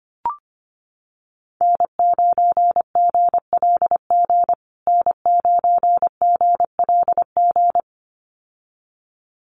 Morse code, heard as a keyed tone, about 25 wpm, 700 Hz. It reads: N9GLG N9GLG
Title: USA Call Signs Repeated 2x 25wpm